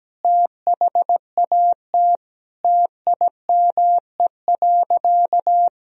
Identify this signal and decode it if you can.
Morse code, heard as a keyed tone, about 17 wpm, 700 Hz. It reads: THAT TIME.
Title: Sentences from Top 1000 Words - Review Part 2 - 17wpm